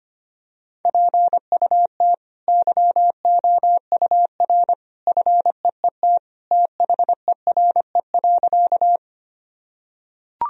Morse code, heard as a keyed tone, about 25 wpm, 700 Hz. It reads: PUT YOUR FEET THERE.
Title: Sentences from Top 1000 Words - Review Part 1 - 25wpm